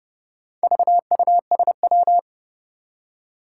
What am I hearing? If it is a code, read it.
Morse code, heard as a keyed tone, about 30 wpm, 700 Hz. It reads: VUSW
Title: Four Challenging Letter-Number Combinations 30wpm